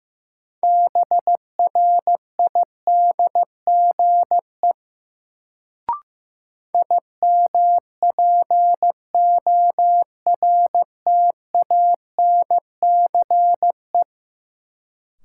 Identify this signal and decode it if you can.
Morse code, heard as a keyed tone, about 15 wpm, 700 Hz. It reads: BRIDGE IMPORTANCE
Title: Sentences from Top 1200 Words - New Word Drill - 15wpm